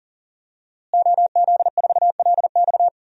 Morse code, heard as a keyed tone, about 37 wpm, 700 Hz. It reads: OZ4LX
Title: DXCC-100 Call Signs - 5 Characters - Repeated 2x - 40wpm